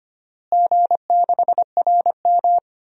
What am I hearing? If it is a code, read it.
Morse code, heard as a keyed tone, about 25 wpm, 700 Hz. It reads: G6RM
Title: DXCC-100 Call Signs - 4 Characters - Repeated 3x - 25wpm